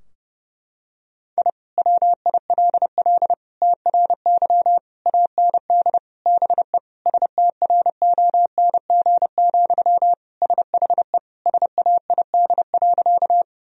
Morse code, heard as a keyed tone, about 30 wpm, 700 Hz. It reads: I WILL TRY AND BE STRONG, SHE SAID.
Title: Sentences from Top 1000 Words - Review Part 3 - 30wpm